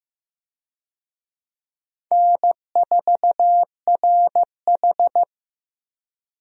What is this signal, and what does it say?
Morse code, heard as a keyed tone, about 15 wpm, 700 Hz. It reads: N4RH
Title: USA Call Signs - 4 Characters - Repeated 2x - 15wpm